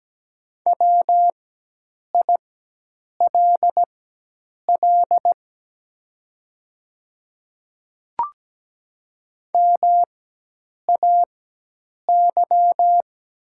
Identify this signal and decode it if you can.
Morse code, heard as a keyed tone, about 17 wpm, spaced as for 8 wpm, 700 Hz. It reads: WILL MAY
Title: 4x Character Spacing - Top 300 Words - 17wpm